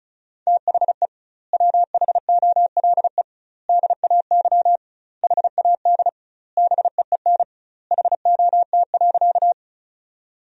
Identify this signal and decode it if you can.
Morse code, heard as a keyed tone, about 36 wpm, 700 Hz. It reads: THE WHOLE DAY HAD BEEN HOT.
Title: 2x Word-Spacing - Sentences from Top 500 Words - 35wpm